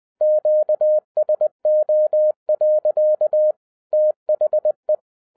Morse code, heard as a keyed tone, about 20 wpm, 610 Hz. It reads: QSO. THE
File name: 2012-01-mp3 / n05-CAME BACK TO HAM RADIO AFTER 30 YEARS OF INTERVAL-20wpm